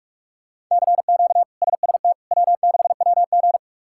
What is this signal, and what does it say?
Morse code, heard as a keyed tone, about 46 wpm, 700 Hz. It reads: CQ SST W6WG
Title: K1USN SST QSO Practice - 45wpm